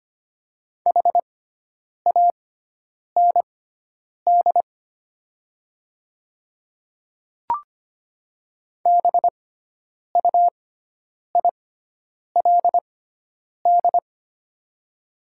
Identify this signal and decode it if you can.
Morse code, heard as a keyed tone, about 25 wpm, spaced as for 9 wpm, 700 Hz. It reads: HAND BUILD
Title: 6x Character Spacing - Top 200 Words - 25wpm